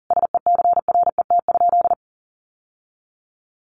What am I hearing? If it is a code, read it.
Morse code, heard as a keyed tone, about 40 wpm, 700 Hz. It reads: SECRET?
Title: Sentences from Top 1200 Words 40wpm